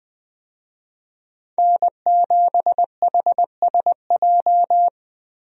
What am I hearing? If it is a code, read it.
Morse code, heard as a keyed tone, about 21 wpm, 700 Hz. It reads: N7HSJ
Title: USA Call Signs - 5 Characters - Repeated 3x - 20wpm